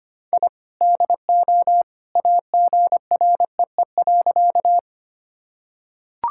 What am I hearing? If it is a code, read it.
Morse code, heard as a keyed tone, about 25 wpm, 700 Hz. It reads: I DO AGREE.
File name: practice / 25wpm / Sentences with 2-Word Phrases - 25wpm